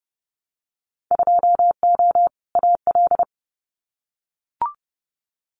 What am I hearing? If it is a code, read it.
Morse code, heard as a keyed tone, about 30 wpm, 700 Hz. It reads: 2O AL
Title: Winter Field Day Exchanges 30wpm